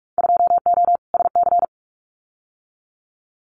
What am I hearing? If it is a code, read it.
Morse code, heard as a keyed tone, about 44 wpm, 700 Hz. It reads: 2O SC
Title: Winter Field Day Exchanges Encore 45wpm